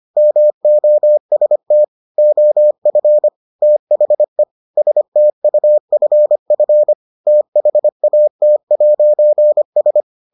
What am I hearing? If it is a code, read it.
Morse code, heard as a keyed tone, about 25 wpm, 600 Hz. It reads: MOST OF THE STUFF THAT'S